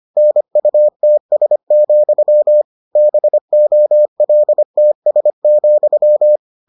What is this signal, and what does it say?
Morse code, heard as a keyed tone, about 25 wpm, 600 Hz. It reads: NUTS, BOLTS,